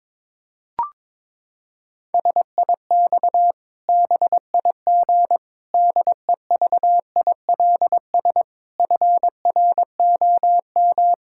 Morse code, heard as a keyed tone, about 22 wpm, 700 Hz. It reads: SIX BIG DEVILS FROM
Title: Pangram Challenge v2 - 22wpm